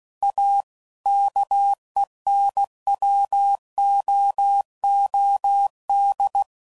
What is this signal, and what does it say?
Morse code, heard as a keyed tone, about 16 wpm, 800 Hz. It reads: A KENWOOD
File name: CW Code Groups @15WPM / QSOs / WD0BGP Spaced